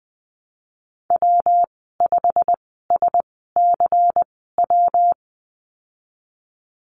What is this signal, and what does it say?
Morse code, heard as a keyed tone, about 20 wpm, spaced as for 14 wpm, 700 Hz. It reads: W5SCW
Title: 122 Lesson 20g - Number 3 - Call Signs 20wpm